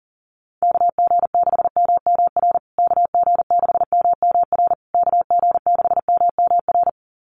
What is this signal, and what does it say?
Morse code, heard as a keyed tone, about 40 wpm, 700 Hz. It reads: KG6MMR KG6MMR KG6MMR
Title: USA Call Signs Repeated 3x 40wpm